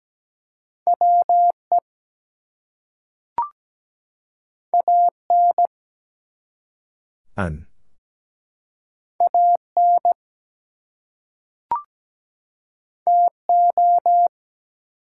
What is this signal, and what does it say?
Morse code, heard as a keyed tone, about 17 wpm, 700 Hz. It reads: WE AN AN TO
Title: Two Letter Words - 17wpm